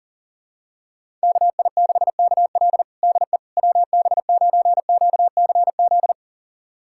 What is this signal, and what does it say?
Morse code, heard as a keyed tone, about 40 wpm, 700 Hz. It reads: KI6KL DE WB9QCZ
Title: USA Callsign DE Callsign 40wpm